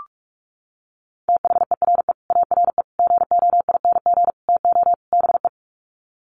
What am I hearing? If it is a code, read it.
Morse code, heard as a keyed tone, about 44 wpm, 700 Hz. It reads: THERE ARE GOING TO BE
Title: Sets of 5 Words 45wpm